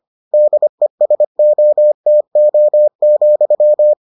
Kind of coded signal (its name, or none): Morse code